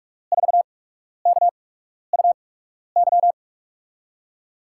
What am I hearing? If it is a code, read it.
Morse code, heard as a keyed tone, about 44 wpm, spaced as for 12 wpm, 700 Hz. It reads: 4KUY
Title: USA Call Signs - 8x Character Spacing - 45wpm